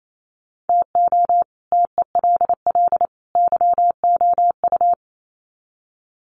Morse code, heard as a keyed tone, about 28 wpm, 700 Hz. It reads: TO TELL YOU
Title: Sets of 3 Words 28wpm